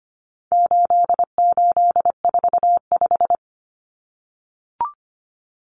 Morse code, heard as a keyed tone, about 25 wpm, 700 Hz. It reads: 8845
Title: Numbers - 4-Digit 25wpm